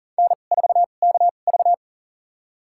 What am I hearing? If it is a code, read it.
Morse code, heard as a keyed tone, about 39 wpm, spaced as for 28 wpm, 700 Hz. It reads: N4KV
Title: USA Call Signs - 4 Characters - 2x Character Spacing - 40wpm